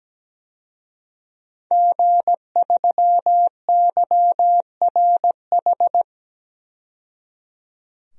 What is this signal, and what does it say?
Morse code, heard as a keyed tone, about 17 wpm, 700 Hz. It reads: G3YRH
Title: DXCC-100 Call Signs - 5 Characters - 17wpm